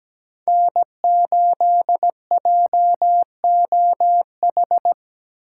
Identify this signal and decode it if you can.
Morse code, heard as a keyed tone, about 20 wpm, 700 Hz. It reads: N8JOH
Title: Speed Racing - USA 1x3 Call Signs 17wpm